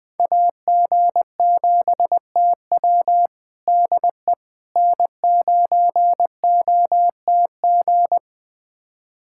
Morse code, heard as a keyed tone, about 21 wpm, 700 Hz. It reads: AG7TW DE N9OTG
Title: USA Callsign DE Callsign - 5 Characters - 20wpm